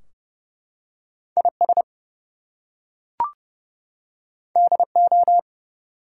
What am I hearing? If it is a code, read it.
Morse code, heard as a keyed tone, about 29 wpm, 700 Hz. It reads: IS DO
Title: Two Letter Words - 30wpm